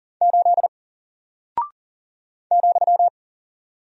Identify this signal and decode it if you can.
Morse code, heard as a keyed tone, about 40 wpm, 700 Hz. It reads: 8 ,
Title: Single Character (Letter:Number:Punctuation:Prosign) Alt 40wpm